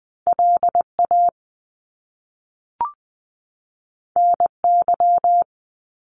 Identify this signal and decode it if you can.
Morse code, heard as a keyed tone, about 20 wpm, 700 Hz. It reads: LA NY
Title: US States 20wpm